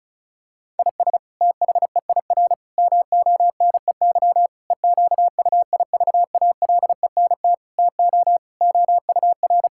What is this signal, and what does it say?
Morse code, heard as a keyed tone, about 34 wpm, 700 Hz. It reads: IS THEIR MONEY EQUIVALENT TO OUR
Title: Sentences from Top 1600 Words 35wpm